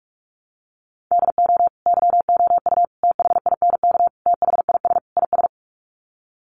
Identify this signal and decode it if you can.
Morse code, heard as a keyed tone, about 44 wpm, 700 Hz. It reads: DO YOU THINK THIS IS
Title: Sets of 5 Words Encore2 45wpm